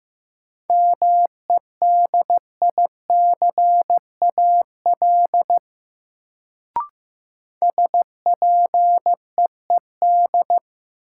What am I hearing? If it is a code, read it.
Morse code, heard as a keyed tone, about 16 wpm, 700 Hz. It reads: MEDICAL SPEED
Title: Sentences from Top 800 Words - New Word Drill - 15wpm